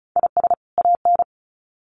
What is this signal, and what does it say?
Morse code, heard as a keyed tone, about 34 wpm, 700 Hz. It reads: IS AN
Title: Sets of 2 Words 35wpm